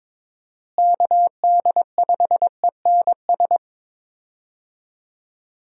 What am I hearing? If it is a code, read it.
Morse code, heard as a keyed tone, about 22 wpm, 700 Hz. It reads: KD5ENS
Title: USA Call Signs - 6 Characters - Repeated 2x - 22wpm